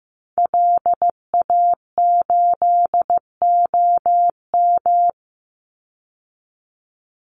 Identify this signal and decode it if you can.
Morse code, heard as a keyed tone, about 15 wpm, 700 Hz. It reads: LA8OM
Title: Call Signs Repeated 3x - 15wpm